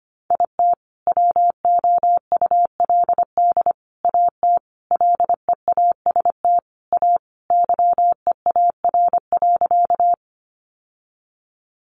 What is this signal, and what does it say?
Morse code, heard as a keyed tone, about 25 wpm, 700 Hz. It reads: IT WOULD AT LEAST A YEAR.
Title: Sentences from Top 700 Words 25wpm